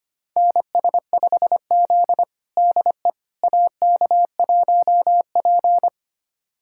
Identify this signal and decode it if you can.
Morse code, heard as a keyed tone, about 25 wpm, 700 Hz. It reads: NS5Z DE AK1P
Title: USA Callsign DE Callsign - 4 Characters - 25wpm